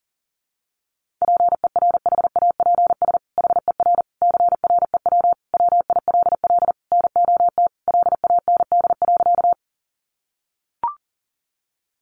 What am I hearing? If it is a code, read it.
Morse code, heard as a keyed tone, about 40 wpm, 700 Hz. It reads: PERHAPS HER CREW WILL NOT LAND.
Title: Sentences from Top 1500 Words 40wpm